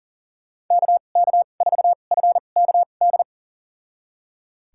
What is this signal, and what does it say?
Morse code, heard as a keyed tone, about 40 wpm, spaced as for 29 wpm, 700 Hz. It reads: KK4FKD
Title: USA Call Signs - 2x Character Spacing - 40wpm